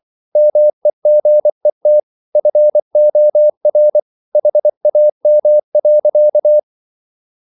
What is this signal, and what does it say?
Morse code, heard as a keyed tone, about 24 wpm, 600 Hz. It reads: MEGET FOR HAM.
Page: Ferie 24 wpm